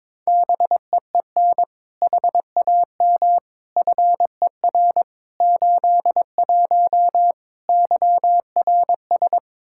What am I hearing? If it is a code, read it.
Morse code, heard as a keyed tone, about 21 wpm, 700 Hz. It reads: BEEN HAM FER 81 YRS
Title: Practice QSO - Exchange 3 - More Info - 22wpm